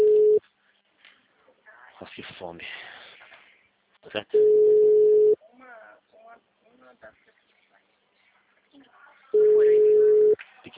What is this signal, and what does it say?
Ringback tone (Europe, 425 Hz), with speech.
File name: out- -2013- - - .92001